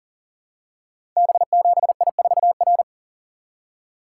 Morse code, heard as a keyed tone, about 38 wpm, 700 Hz. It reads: B7I4R
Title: Code Groups - 5 Letter:Numbers - 40wpm